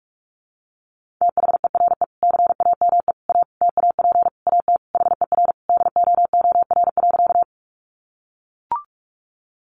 Morse code, heard as a keyed tone, about 44 wpm, 700 Hz. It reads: THERE CAME A TAP AT HER DOOR.
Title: Sentences from Top 1400 Words 45wpm